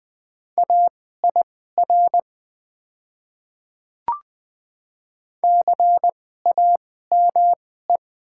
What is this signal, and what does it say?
Morse code, heard as a keyed tone, about 20 wpm, spaced as for 14 wpm, 700 Hz. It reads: AIR CAME
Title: 2x Character Spacing - Top 200 Words - 20wpm